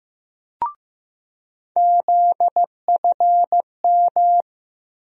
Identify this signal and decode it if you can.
Morse code, heard as a keyed tone, about 15 wpm, 700 Hz. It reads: ZFM
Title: USA Call Sign Postfixes - 3 Letters 15wpm